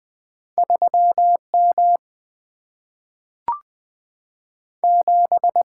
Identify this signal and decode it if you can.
Morse code, heard as a keyed tone, about 20 wpm, 700 Hz. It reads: 3M 7
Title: Single-Letter Single-Number 20wpm